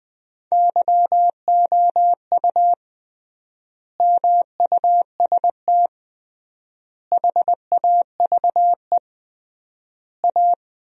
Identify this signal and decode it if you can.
Morse code, heard as a keyed tone, about 21 wpm, 700 Hz. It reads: YOU MUST HAVE A
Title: 3x Word-Spacing - Sentences from Top 500 Words - 20wpm